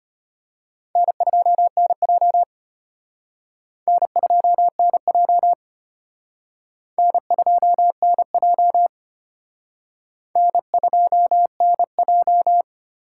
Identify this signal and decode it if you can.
Morse code, heard as a keyed tone, about 32 wpm, 700 Hz. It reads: N2NJ N2NJ N2NJ N2NJ